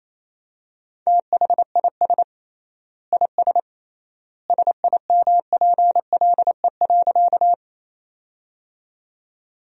Morse code, heard as a keyed tone, about 27 wpm, 700 Hz. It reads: THIS IS SIMPLE.
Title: 3x Word-Spacing - Sentences from Top 400 Words - 28wpm